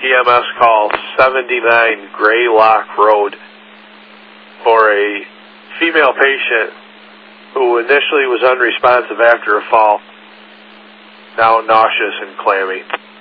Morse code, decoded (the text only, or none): none